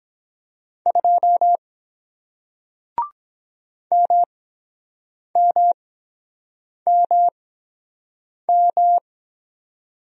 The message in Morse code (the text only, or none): 2 MMMM